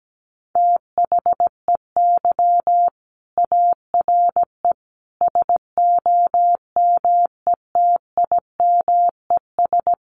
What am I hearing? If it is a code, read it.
Morse code, heard as a keyed tone, about 17 wpm, 700 Hz. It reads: THEY ARE SOMETIMES
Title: Sentences from Top 1300 Words 17wpm